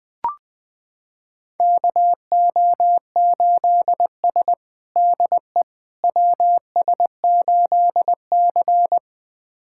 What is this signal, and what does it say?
Morse code, heard as a keyed tone, about 21 wpm, 700 Hz. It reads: KO8S DE WS8C
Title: USA Callsign DE Callsign - 4 Characters - 20wpm